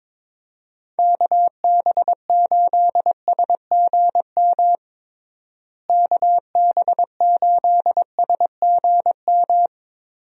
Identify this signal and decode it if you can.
Morse code, heard as a keyed tone, about 22 wpm, 700 Hz. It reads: KB8SGM KB8SGM
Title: USA Call Signs - 6 Characters - Repeated 3x - 22wpm